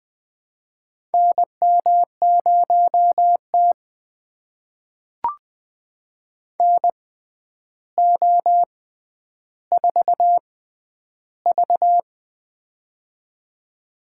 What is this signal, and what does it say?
Morse code, heard as a keyed tone, about 20 wpm, spaced as for 10 wpm, 700 Hz. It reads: NM0T NO4V
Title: USA Call Signs - 4 Characters - 6x Character Spacing - 20wpm